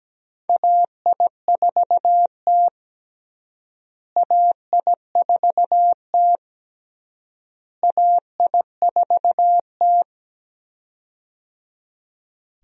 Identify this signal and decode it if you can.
Morse code, heard as a keyed tone, about 17 wpm, 700 Hz. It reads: AI4T AI4T AI4T
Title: USA Call Signs - 4 Characters - Repeated 3x - 17wpm